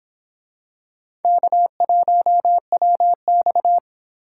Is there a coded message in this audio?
Yes: Morse code from 1.2 to 3.8 s, heard as a keyed tone, about 26 wpm, 700 Hz. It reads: K1WX